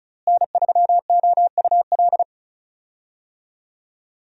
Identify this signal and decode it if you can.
Morse code, heard as a keyed tone, about 34 wpm, 700 Hz. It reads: N3OUL
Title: USA Call Signs - 5 Characters - Repeated 3x - 35wpm